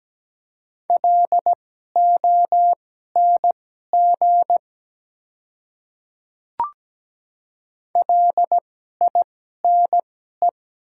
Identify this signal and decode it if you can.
Morse code, heard as a keyed tone, about 17 wpm, spaced as for 12 wpm, 700 Hz. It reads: LONG LINE
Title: 2x Character Spacing - Top 200 Words - 17wpm